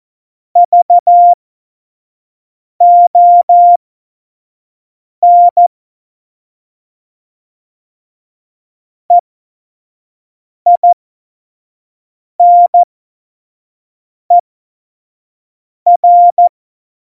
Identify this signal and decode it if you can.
Morse code, heard as a keyed tone, about 14 wpm, spaced as for 5 wpm, 700 Hz. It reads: VON EINER
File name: Morsekurs / Witze / Witz 12. Dezember 2017, 25 F70